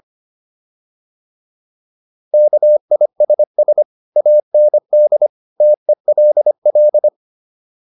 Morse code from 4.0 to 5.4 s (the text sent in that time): AND